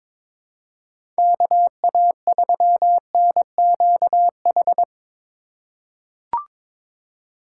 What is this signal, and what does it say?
Morse code, heard as a keyed tone, about 22 wpm, 700 Hz. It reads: KA3NQH
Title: USA Call Signs - 6 Characters - 3x Character Spacing - 22wpm